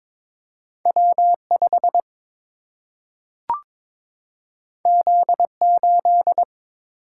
Morse code, heard as a keyed tone, about 22 wpm, 700 Hz. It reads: W5 Z8
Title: Single-Letter Single-Number 22wpm